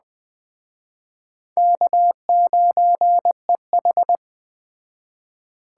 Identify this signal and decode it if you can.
Morse code, heard as a keyed tone, about 20 wpm, 700 Hz. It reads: K9EH